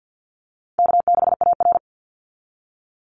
Morse code, heard as a keyed tone, about 50 wpm, 700 Hz. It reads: K6AR
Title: Call Signs - 50wpm